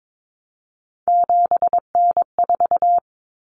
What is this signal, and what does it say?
Morse code, heard as a keyed tone, about 22 wpm, 700 Hz. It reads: 7N4
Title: Common Cut Numbers - 3-Digit 22wpm